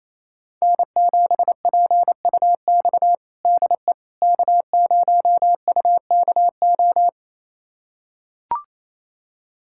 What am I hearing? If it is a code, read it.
Morse code, heard as a keyed tone, about 28 wpm, 700 Hz. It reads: N7PUX DE K0UKO
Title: USA Callsign DE Callsign - 5 Characters - 28wpm